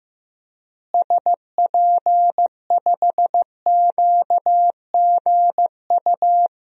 Morse code, heard as a keyed tone, about 16 wpm, 700 Hz. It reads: SP5QGU
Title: DXCC-100 Call Signs - 6 Characters - Repeated 3x - 15wpm